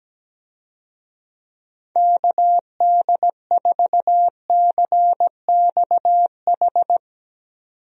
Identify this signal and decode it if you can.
Morse code, heard as a keyed tone, about 17 wpm, 700 Hz. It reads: KD4CXH
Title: USA Call Signs - 6 Characters - 17wpm